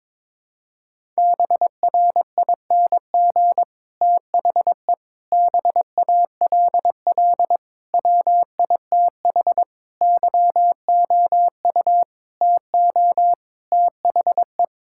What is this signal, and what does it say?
Morse code, heard as a keyed tone, about 22 wpm, 700 Hz. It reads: BRING THE BALL WITH YOU TO THE
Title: Sentences from Top 1000 Words - Review Part 6 - 22wpm